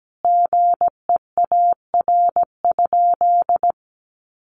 Morse code, heard as a keyed tone, about 17 wpm, 700 Hz. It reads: GEAR?
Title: Sentences from Top 1400 Words 17wpm